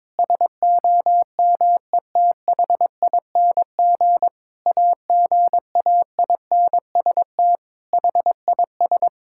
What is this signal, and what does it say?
Morse code, heard as a keyed tone, about 21 wpm, 700 Hz. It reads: SOMETHING AGAINST HIS
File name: Sentences with 9-Word Phrases - 22wpm